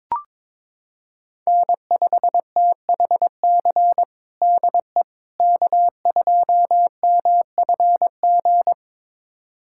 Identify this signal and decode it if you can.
Morse code, heard as a keyed tone, about 21 wpm, 700 Hz. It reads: N5THC DE K2MFG
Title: USA Callsign DE Callsign - 5 Characters - 22wpm